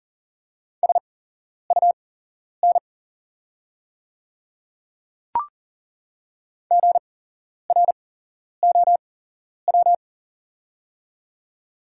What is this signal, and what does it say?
Morse code, heard as a keyed tone, about 40 wpm, spaced as for 11 wpm, 690 Hz. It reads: SUN GROW
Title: 8x Character Spacing - Top 300 Words - 40wpm